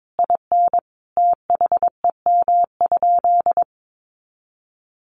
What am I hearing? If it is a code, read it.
Morse code, heard as a keyed tone, about 22 wpm, 700 Hz. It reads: IN THEM?
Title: Sentences from Top 900 Words 22wpm